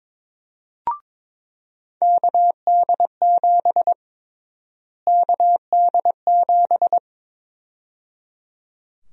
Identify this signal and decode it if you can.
Morse code, heard as a keyed tone, about 22 wpm, 700 Hz. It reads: KD7 KD7
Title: USA Call Sign Prefixes - Repeated 2x - 22wpm